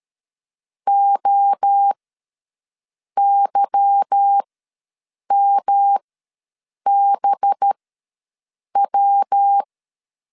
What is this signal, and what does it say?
Morse code, heard as a keyed tone, about 13 wpm, spaced as for 6 wpm, 800 Hz. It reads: OYMBW
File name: LEZIONI / 8 YJ45 / 8 YJ ws 12wpm 3342